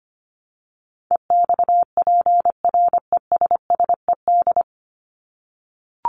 Morse code, heard as a keyed tone, about 25 wpm, 700 Hz. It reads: EXPRESSED
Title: ED Suffix - Top 100 Words with Base Words 25wpm